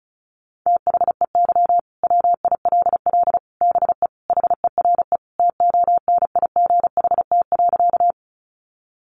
Morse code, heard as a keyed tone, about 34 wpm, 700 Hz. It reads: THEY WILL BE HERE TONIGHT.
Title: Sentences from Top 600 Words 35wpm